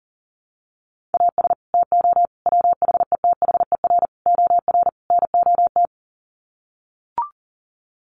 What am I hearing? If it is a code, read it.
Morse code, heard as a keyed tone, about 40 wpm, 700 Hz. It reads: AS TO WHETHER OR NOT
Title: Sets of 5 Words Encore 40wpm